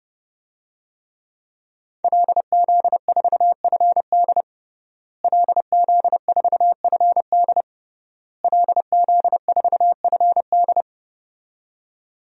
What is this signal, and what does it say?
Morse code, heard as a keyed tone, about 30 wpm, 700 Hz. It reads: LZ4FD LZ4FD LZ4FD
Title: DXCC-100 Call Signs - 5 Characters - Repeated 3x - 30wpm